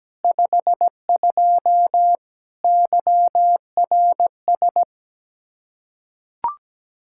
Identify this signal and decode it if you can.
Morse code, heard as a keyed tone, about 17 wpm, 700 Hz. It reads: 52 YRS
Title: Practice QSO - Exchange 3 - More Info - 17wpm